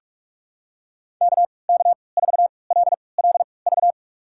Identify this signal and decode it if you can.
Morse code, heard as a keyed tone, about 46 wpm, spaced as for 25 wpm, 700 Hz. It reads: KK4LLV
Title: USA Call Signs - 6 Characters - 3x Character Spacing - 45wpm